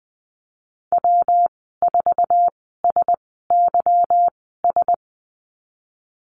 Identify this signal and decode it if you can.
Morse code, heard as a keyed tone, about 20 wpm, spaced as for 14 wpm, 700 Hz. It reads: W4SYS
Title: 154 Lesson 25g - Letter p - Call Signs 20wpm